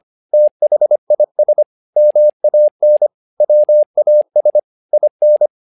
Morse code, heard as a keyed tone, about 25 wpm, 600 Hz. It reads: THIS MAN WAS IN
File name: learning center / tower trouble / tower trouble 25wpm